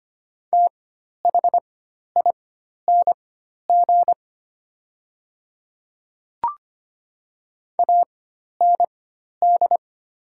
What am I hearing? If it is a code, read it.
Morse code, heard as a keyed tone, about 25 wpm, spaced as for 12 wpm, 700 Hz. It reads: THING AND
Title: 4x Character Spacing - Top 100 Words - 25wpm